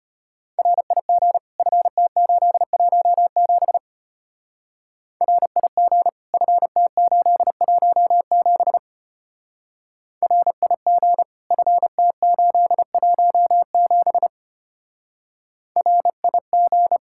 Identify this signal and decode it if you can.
Morse code, heard as a keyed tone, about 31 wpm, 700 Hz. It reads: RIG FT817 RIG FT817 RIG FT817 RIG
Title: Speed-Racing Common QSO Phrases Encore - 25wpm